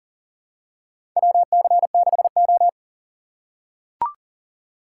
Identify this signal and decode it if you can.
Morse code, heard as a keyed tone, about 40 wpm, 700 Hz. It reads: WC6O
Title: USA Call Signs - 3x Character Spacing - 40wpm